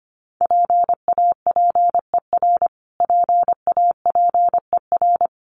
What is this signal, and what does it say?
Morse code, heard as a keyed tone, about 25 wpm, 700 Hz. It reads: PAPER PAPER
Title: Top 500 Words Repeated 25wpm